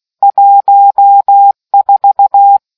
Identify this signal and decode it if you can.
Morse code, heard as a keyed tone, about 16 wpm, 800 Hz. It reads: 14